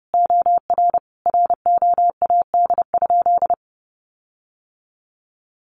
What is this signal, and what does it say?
Morse code, heard as a keyed tone, about 30 wpm, 700 Hz. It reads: OR ROAD?
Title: Sentences from Top 1400 Words 30wpm